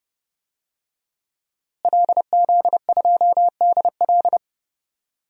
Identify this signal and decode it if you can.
Morse code, heard as a keyed tone, about 32 wpm, 700 Hz. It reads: LZ2DL
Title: DXCC-100 Call Signs - 5 Characters - Repeated 3x - 30wpm